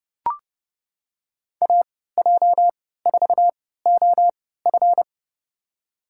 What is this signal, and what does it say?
Morse code, heard as a keyed tone, about 30 wpm, spaced as for 17 wpm, 700 Hz. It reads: AJ4OF
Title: USA Call Signs - 5 Characters - 3x Character Spacing - 30wpm